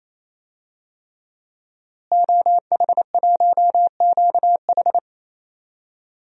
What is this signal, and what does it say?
Morse code, heard as a keyed tone, about 28 wpm, 700 Hz. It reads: OH1QH
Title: DXCC-100 Call Signs - 5 Characters - Repeated 3x - 28wpm